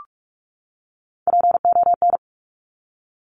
Morse code, heard as a keyed tone, about 46 wpm, 700 Hz. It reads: PON
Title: Top Three Letter Combinations in Words 45wpm